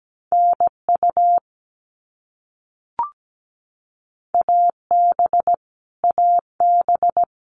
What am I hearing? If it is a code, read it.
Morse code, heard as a keyed tone, about 17 wpm, 700 Hz. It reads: NU AB AB
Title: Canadian Provinces and Territories - Repeated 3x - 17wpm